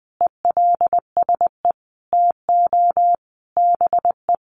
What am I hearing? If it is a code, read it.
Morse code, heard as a keyed tone, about 20 wpm, 700 Hz. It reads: ELSE TO BE
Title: Sentences from Top 800 Words 20wpm